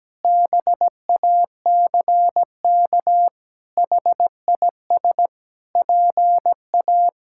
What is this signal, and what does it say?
Morse code, heard as a keyed tone, about 17 wpm, 700 Hz. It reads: BACK HIS PA
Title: Pangram Challenge v2 - 17wpm